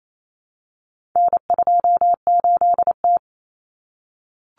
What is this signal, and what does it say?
Morse code, heard as a keyed tone, about 28 wpm, 700 Hz. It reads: N28T
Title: Common Cut Numbers - 4-Digit 28wpm